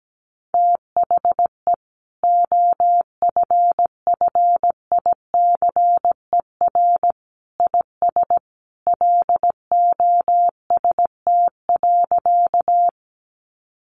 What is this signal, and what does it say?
Morse code, heard as a keyed tone, about 17 wpm, 700 Hz. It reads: THE OFFICER IS LOST.
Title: Sentences from Top 1400 Words 17wpm